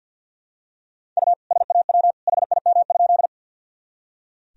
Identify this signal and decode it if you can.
Morse code, heard as a keyed tone, about 46 wpm, 700 Hz. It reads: U SAW HIM?
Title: Sentences from Top 300 Words and Abbreviations 50wpm